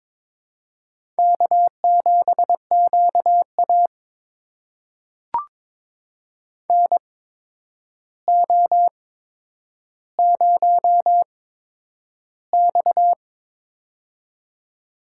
Morse code, heard as a keyed tone, about 22 wpm, spaced as for 9 wpm, 700 Hz. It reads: K7QA NO0X
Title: USA Call Signs - 4 Characters - 8x Character Spacing - 22wpm